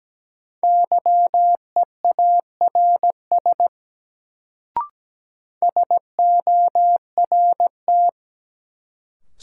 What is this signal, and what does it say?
Morse code, heard as a keyed tone, about 17 wpm, 700 Hz. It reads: YEARS SORT
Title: Sets of 2 Words - New Word Drill - 17wpm